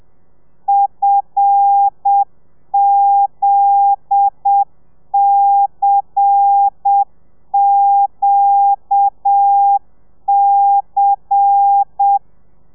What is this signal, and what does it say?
Morse code, heard as a keyed tone, about 7 wpm, 800 Hz. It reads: FZCQC
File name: telegrafia / i2tfj / Lez10 30